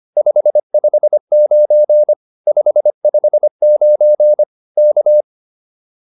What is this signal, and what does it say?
Morse code, heard as a keyed tone, about 25 wpm, 600 Hz. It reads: 559 559 K
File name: POTA310a 25